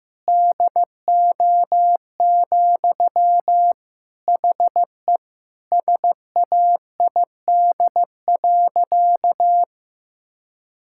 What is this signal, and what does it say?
Morse code, heard as a keyed tone, about 15 wpm, 700 Hz. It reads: DO, HE SAID.
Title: Sentences from Top 1600 Words 15wpm